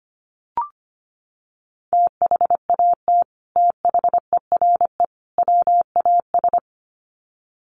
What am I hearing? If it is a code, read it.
Morse code, heard as a keyed tone, about 25 wpm, 700 Hz. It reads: THAT THERE WAS
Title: Sets of 3 Words 25wpm